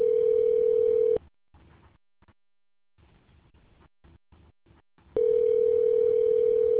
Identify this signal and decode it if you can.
Ringback tone (North America), 440 + 480 Hz.